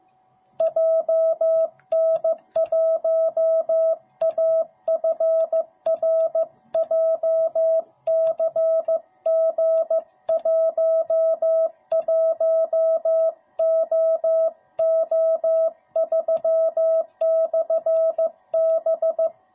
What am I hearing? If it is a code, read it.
Morse code, heard as a keyed tone, about 15 wpm, 660 Hz. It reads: JN1AFR JCG11OO3/B